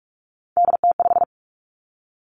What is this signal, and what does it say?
Morse code, heard as a keyed tone, about 46 wpm, 700 Hz. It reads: DT5
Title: Three Challenging Letter-Number Combinations 45wpm